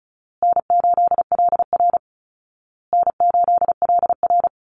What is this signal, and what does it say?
Morse code, heard as a keyed tone, about 34 wpm, 700 Hz. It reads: N8LR N8LR
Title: Easy Call Signs Repeated 3x - 35wpm